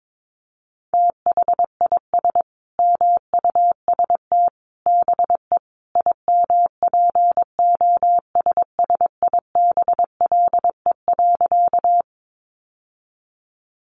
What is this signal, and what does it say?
Morse code, heard as a keyed tone, about 22 wpm, 700 Hz. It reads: THIS MUST BE IMPOSSIBLE.
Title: Sentences from Top 1100 Words 22wpm